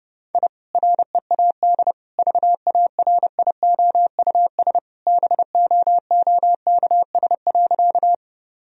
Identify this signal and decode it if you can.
Morse code, heard as a keyed tone, about 30 wpm, 700 Hz. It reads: I READ VARIOUS BOOKS.
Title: Sentences from Top 1000 Words - Review Part 4 - 30wpm